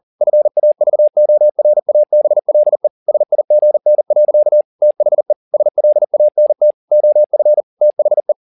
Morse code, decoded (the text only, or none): FAVORABLE SIGN. THE SLANT OF THE